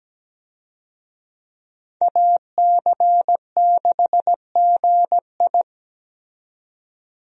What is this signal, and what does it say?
Morse code, heard as a keyed tone, about 17 wpm, 700 Hz. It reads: AC6GI
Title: USA Call Signs - 5 Characters - 17wpm